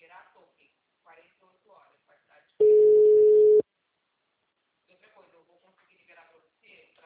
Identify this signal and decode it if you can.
Ringback tone (Europe, 425 Hz), with speech.
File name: out- -2018- - - .32111